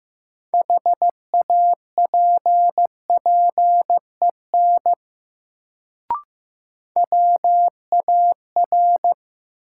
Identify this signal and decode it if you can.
Morse code, heard as a keyed tone, about 15 wpm, 700 Hz. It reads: HAPPEN WAR
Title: Sentences from Top 400 Words - New Word Drill - 15wpm